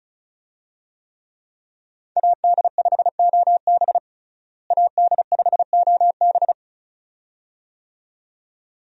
Morse code, heard as a keyed tone, about 36 wpm, 700 Hz. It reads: AD5OB AD5OB
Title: USA Call Signs - 5 Characters - Repeated 2x - 35wpm